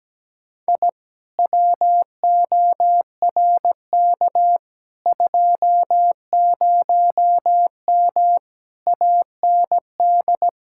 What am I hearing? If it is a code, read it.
Morse code, heard as a keyed tone, about 17 wpm, 700 Hz. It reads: I WORK 20M AND
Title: Speed-Racing Common QSO Phrases - 17wpm